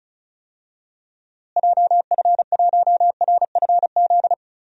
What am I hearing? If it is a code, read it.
Morse code, heard as a keyed tone, about 36 wpm, 700 Hz. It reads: JF1RFZ
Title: DXCC-100 Call Signs - 6 Characters - Repeated 2x - 35wpm